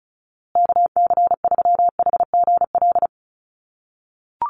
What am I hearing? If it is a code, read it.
Morse code, heard as a keyed tone, about 34 wpm, 700 Hz. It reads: KC3HGL
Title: USA Call Signs Repeated 2x 35wpm